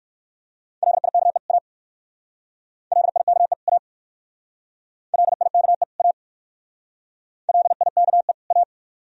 Morse code, heard as a keyed tone, about 54 wpm, 700 Hz. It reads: LIKE A LIKE A LIKE A LIKE A
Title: Speed-Racing - Sets of 2 Words 45wpm